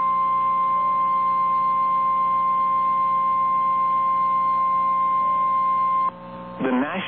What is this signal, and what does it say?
Emergency Alert System (SAME) tones (the 1050 Hz attention tone), with speech. It is heard from 0.8 to 6.1 s.
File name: eas / KNCYKBIE / KNCYKBIE